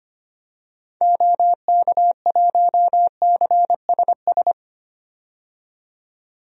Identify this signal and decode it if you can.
Morse code, heard as a keyed tone, about 25 wpm, 700 Hz. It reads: OK1CSS
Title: DXCC-100 Call Signs - 6 Characters - 25wpm